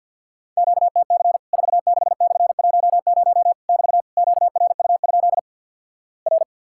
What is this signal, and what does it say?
Morse code, heard as a keyed tone, about 50 wpm, 700 Hz. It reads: QTC 46/10 <BT> QRU? R
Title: Worked All Europe DX Contest - QTC Exchange with Repeats - 50wpm